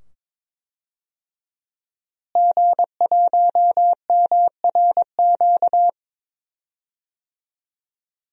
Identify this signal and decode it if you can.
Morse code, heard as a keyed tone, about 22 wpm, 700 Hz. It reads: G1MRQ
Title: DXCC-100 Call Signs - 5 Characters - Repeated 2x - 22wpm